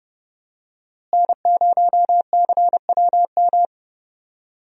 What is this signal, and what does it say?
Morse code, heard as a keyed tone, about 32 wpm, 700 Hz. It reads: N0CWM
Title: USA Call Signs - 5 Characters - Repeated 3x - 30wpm